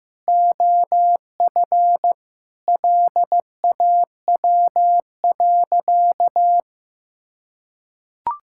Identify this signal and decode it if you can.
Morse code, heard as a keyed tone, about 15 wpm, 700 Hz. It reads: OF LAW.
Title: Sentences from Top 1000 Words - Review Part 5 - 15wpm